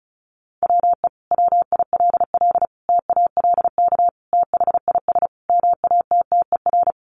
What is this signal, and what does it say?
Morse code, heard as a keyed tone, about 34 wpm, 700 Hz. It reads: WE WILL TALK THIS MATTER
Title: Sentences from Top 700 Words 35wpm